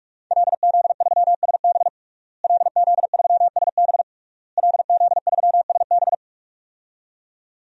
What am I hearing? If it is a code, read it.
Morse code, heard as a keyed tone, about 46 wpm, 700 Hz. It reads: LZ3SB LZ3SB LZ3SB
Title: DXCC-100 Call Signs - 5 Characters - Repeated 3x - 45wpm